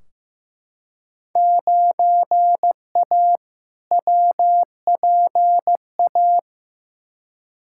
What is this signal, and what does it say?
Morse code, heard as a keyed tone, about 15 wpm, 700 Hz. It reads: 9A WPA
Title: Field Day Exchanges Encore - 15wpm